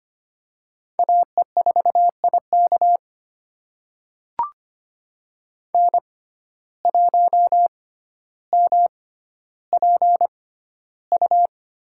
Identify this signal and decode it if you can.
Morse code, heard as a keyed tone, about 25 wpm, spaced as for 13 wpm, 700 Hz. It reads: AE4IK N1MPU
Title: USA Call Signs - 5 Characters - 6x Character Spacing - 25wpm